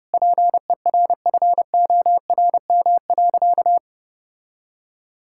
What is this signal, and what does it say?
Morse code, heard as a keyed tone, about 30 wpm, 700 Hz. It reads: PERFORM.
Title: Sentences from Top 1000 Words - Review Part 5 - 30wpm